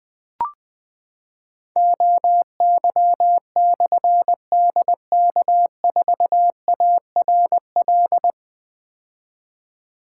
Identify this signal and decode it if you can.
Morse code, heard as a keyed tone, about 20 wpm, 700 Hz. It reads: OY/DK4ARL
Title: Difficult Call Signs 20wpm